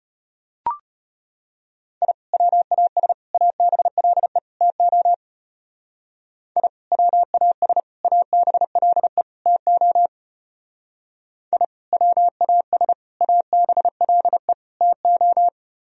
Morse code, heard as a keyed tone, about 32 wpm, 690 Hz. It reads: I WAS ABLE TO I WAS ABLE TO I WAS ABLE TO